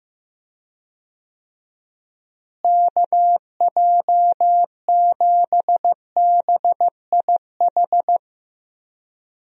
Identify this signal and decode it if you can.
Morse code, heard as a keyed tone, about 15 wpm, 700 Hz. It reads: KJ7BIH
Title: USA Call Signs - 6 Characters - Repeated 3x - 15wpm